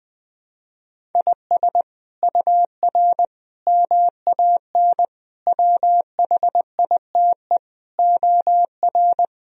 Morse code, heard as a keyed tone, about 20 wpm, 700 Hz. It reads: IS UR MAN WHITE OR
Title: Sentences from Top 400 Words and Abbreviations 20wpm